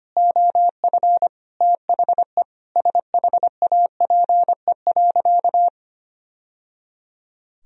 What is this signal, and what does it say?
Morse code, heard as a keyed tone, about 25 wpm, 700 Hz. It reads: OF THE SHAPE.
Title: Sentences from Top 1000 Words - Review Part 3 - 25wpm